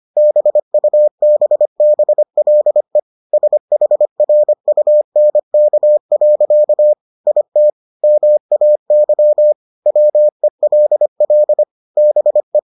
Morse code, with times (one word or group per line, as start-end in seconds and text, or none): 0.2-3.0 BUBBLE
3.3-6.9 SHRUNK.
7.3-7.7 IT
8.0-9.5 MAY
9.9-11.6 WELL
12.0-12.6 BE